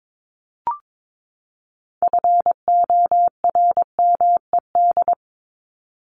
Morse code, heard as a keyed tone, about 22 wpm, 700 Hz. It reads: FORMED
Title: ED Suffix - Top 100 Words with Base Words 22wpm